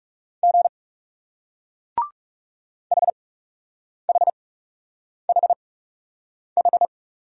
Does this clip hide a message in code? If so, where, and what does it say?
Yes: Morse code from 0.4 to 6.9 s, heard as a keyed tone, about 39 wpm, spaced as for 8 wpm, 700 Hz. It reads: G HHHH